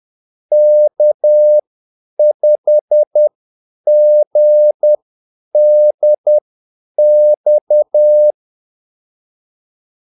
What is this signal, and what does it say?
Morse code, heard as a keyed tone, about 10 wpm, spaced as for 8 wpm, 600 Hz. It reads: K5GDX
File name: Callsigns-10-8wpm